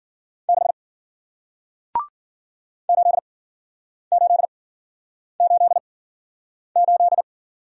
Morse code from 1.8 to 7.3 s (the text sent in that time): 8888